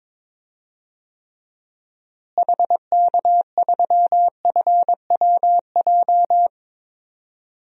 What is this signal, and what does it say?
Morse code, heard as a keyed tone, about 22 wpm, 700 Hz. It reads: HK3FWJ
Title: DXCC-100 Call Signs - 6 Characters - Repeated 2x - 22wpm